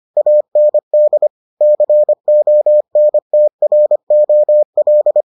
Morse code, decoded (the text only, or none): AND CONTROL